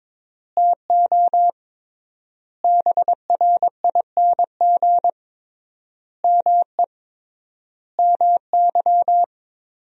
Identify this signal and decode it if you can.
Morse code, heard as a keyed tone, about 22 wpm, 700 Hz. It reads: TO BRING ME MY
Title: 3x Word-Spacing - Sentences from Top 500 Words - 22wpm